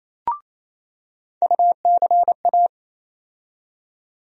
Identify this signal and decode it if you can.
Morse code, heard as a keyed tone, about 28 wpm, 700 Hz. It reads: UCA
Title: USA Call Sign Postfixes - 3 Letters 28wpm